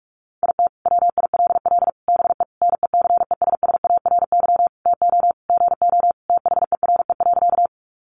Morse code, heard as a keyed tone, about 44 wpm, 700 Hz. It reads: IT WILL BE NECESSARY TO GO THERE.
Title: Sentences from Top 800 Words 45wpm